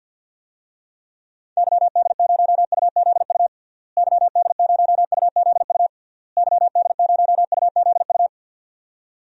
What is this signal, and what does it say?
Morse code, heard as a keyed tone, about 49 wpm, 700 Hz. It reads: YD0UZU YD0UZU YD0UZU
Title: DXCC-100 Call Signs - 6 Characters - Repeated 3x - 50wpm